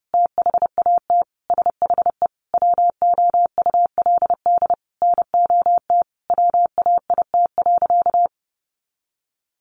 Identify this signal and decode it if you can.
Morse code, heard as a keyed tone, about 30 wpm, 700 Hz. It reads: THAT SHE WOULD NOT WAIT.
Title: Sentences from Top 1400 Words 30wpm